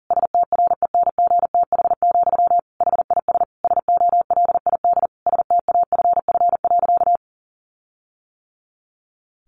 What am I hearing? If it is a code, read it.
Morse code, heard as a keyed tone, about 40 wpm, 700 Hz. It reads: STRENGTH, HIS SOLID STAFF.
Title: Sentences from Top 1500 Words 40wpm